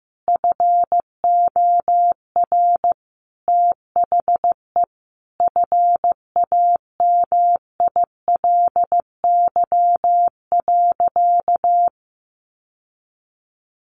Morse code, heard as a keyed tone, about 15 wpm, 700 Hz. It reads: FOR THE FAMILY.
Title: Sentences from Top 1400 Words 15wpm